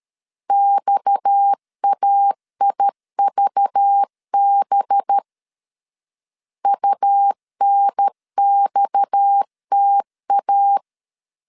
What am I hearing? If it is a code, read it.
Morse code, heard as a keyed tone, about 13 wpm, 800 Hz. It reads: XAIVB UNXTA
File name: LEZIONI / 6 PKX0 / 6 PKX 12wpm 2024